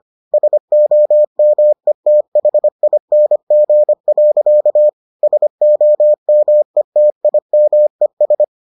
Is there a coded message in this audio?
Yes: Morse code from 0.3 to 8.4 s, heard as a keyed tone, about 25 wpm, 600 Hz. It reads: SOMETHING. SOMETIMES